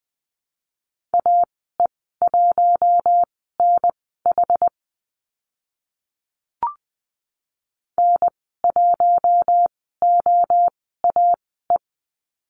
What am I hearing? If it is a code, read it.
Morse code, heard as a keyed tone, about 20 wpm, spaced as for 14 wpm, 700 Hz. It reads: AE1NH N1OAE
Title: 069 Lesson 12g - Letter d - Call Signs 20wpm